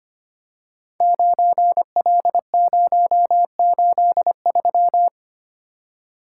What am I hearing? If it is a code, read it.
Morse code, heard as a keyed tone, about 25 wpm, 700 Hz. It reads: 9L083
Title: Code Groups - 5 Letter:Numbers - 25wpm